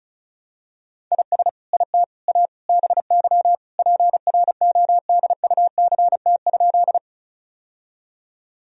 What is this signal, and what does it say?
Morse code, heard as a keyed tone, about 36 wpm, 700 Hz. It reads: IS IT A BY PRODUCT?
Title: Sentences from Top 1000 Words - Review Part 2 - 35wpm